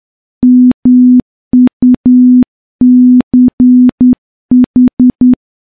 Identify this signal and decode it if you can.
Morse code, heard as a keyed tone, about 10 wpm, 250 Hz. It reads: MUCH
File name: morse-training-10WPM